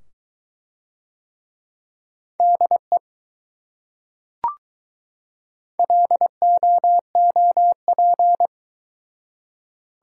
Morse code, heard as a keyed tone, about 21 wpm, 700 Hz. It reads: DE LOOP